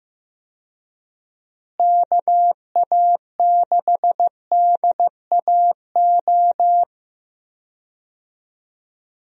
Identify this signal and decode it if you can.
Morse code, heard as a keyed tone, about 15 wpm, 700 Hz. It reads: KA6DAO
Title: USA Call Signs - 6 Characters - Repeated 2x - 15wpm